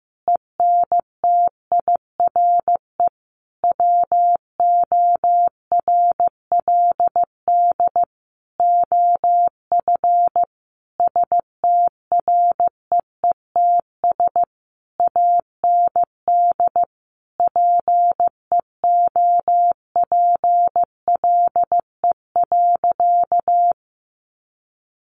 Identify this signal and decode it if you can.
Morse code, heard as a keyed tone, about 15 wpm, 700 Hz. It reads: ENTIRE WORLD OF STREETS AND PEOPLE.
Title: A Sound Of Thunder 15wpm